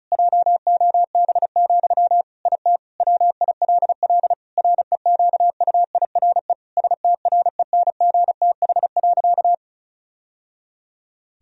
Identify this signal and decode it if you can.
Morse code, heard as a keyed tone, about 36 wpm, 700 Hz. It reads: JOB, IT WILL REQUIRE STRENGTH.
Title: Sentences from Top 1000 Words - Review Part 4 - 35wpm